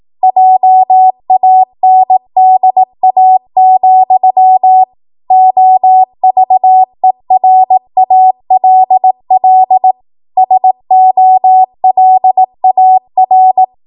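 Morse code, heard as a keyed tone, about 18 wpm, 750 Hz. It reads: JANDA, OVERALL SOLAR
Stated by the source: ARRL 18 wpm Morse code practice archive transmission.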